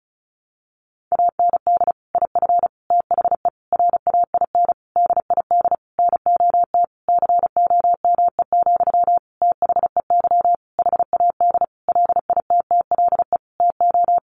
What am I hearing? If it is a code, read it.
Morse code, heard as a keyed tone, about 34 wpm, 700 Hz. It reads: AND IF THE RAIN DID NOT COME, THEY HAD LITTLE TO
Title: Sentences from Top 1300 Words 35wpm